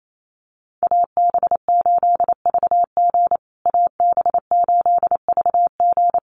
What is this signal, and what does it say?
Morse code, heard as a keyed tone, about 28 wpm, 700 Hz. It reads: AB8VG AB8VG
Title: USA Call Signs Repeated 3x 28wpm